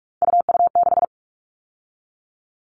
Morse code, heard as a keyed tone, about 44 wpm, 700 Hz. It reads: UU6
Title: All Cut Numbers - 3-Digit 45wpm